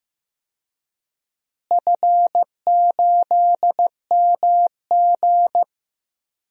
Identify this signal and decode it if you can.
Morse code, heard as a keyed tone, about 15 wpm, 700 Hz. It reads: F8MG
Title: DXCC-100 Call Signs - 4 Characters - Repeated 3x - 15wpm